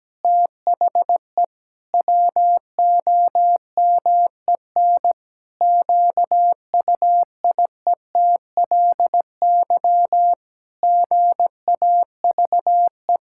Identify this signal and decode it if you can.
Morse code, heard as a keyed tone, about 17 wpm, 700 Hz. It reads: THE WOMEN QUIETLY GAVE
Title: Pangram Challenge v2 - 17wpm